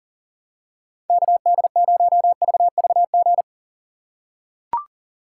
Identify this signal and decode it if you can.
Morse code, heard as a keyed tone, about 40 wpm, 700 Hz. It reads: KD0VVG
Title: USA Call Signs - 6 Characters - 2x Character Spacing - 40wpm